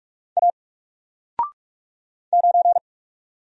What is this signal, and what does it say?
Morse code, heard as a keyed tone, about 46 wpm, 700 Hz. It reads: A 9
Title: Single Letter-Number Alt 45wpm